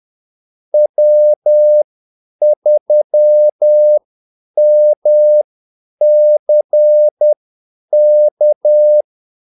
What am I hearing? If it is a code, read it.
Morse code, heard as a keyed tone, about 10 wpm, spaced as for 8 wpm, 600 Hz. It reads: W3MCK
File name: Callsigns-10-8wpm